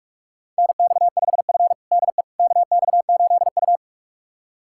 Morse code, heard as a keyed tone, about 47 wpm, 700 Hz. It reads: NX5F DE KX8U
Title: USA Callsign DE Callsign - 4 Characters - 45wpm